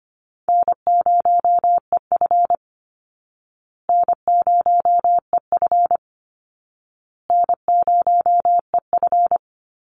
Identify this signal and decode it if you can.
Morse code, heard as a keyed tone, about 25 wpm, 700 Hz. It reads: N0EF N0EF N0EF
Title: Easy Call Signs Repeated 3x - 25wpm